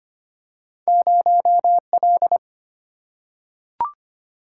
Morse code, heard as a keyed tone, about 25 wpm, 700 Hz. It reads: 0L
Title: Single-Letter Single-Number 25wpm